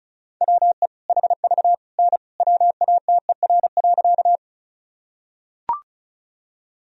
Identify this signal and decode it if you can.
Morse code, heard as a keyed tone, about 36 wpm, 700 Hz. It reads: WE HV N WATER.
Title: Sentences from Top 100 Words and Abbreviations 35wpm